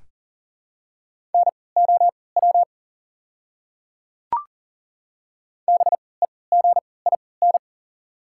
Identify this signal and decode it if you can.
Morse code, heard as a keyed tone, about 40 wpm, spaced as for 23 wpm, 700 Hz. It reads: NOW BEGIN
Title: 3x Character Spacing - Top 300 Words - 40wpm